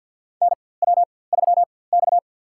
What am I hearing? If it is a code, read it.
Morse code, heard as a keyed tone, about 50 wpm, spaced as for 23 wpm, 700 Hz. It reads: NW3X
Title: USA Call Signs - 4x Character Spacing - 50wpm